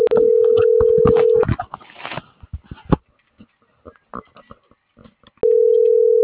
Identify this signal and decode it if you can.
Ringback tone (North America), 440 + 480 Hz.